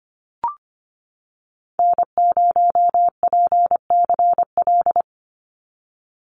Morse code, heard as a keyed tone, about 25 wpm, 690 Hz. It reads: N0PCL
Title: NA SOTA Activator Callsigns 25wpm